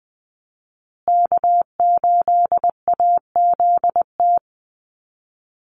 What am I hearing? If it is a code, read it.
Morse code, heard as a keyed tone, about 20 wpm, 700 Hz. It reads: K8AZT
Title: Call Signs Repeated 2x - 20wpm